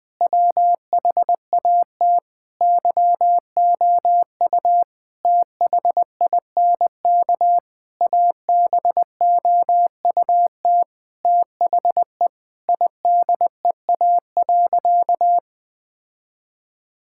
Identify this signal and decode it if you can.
Morse code, heard as a keyed tone, about 19 wpm, 700 Hz. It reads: WHAT YOU THINK ABOUT THE IDEA.
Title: Sentences with 6-Word Phrases - 20wpm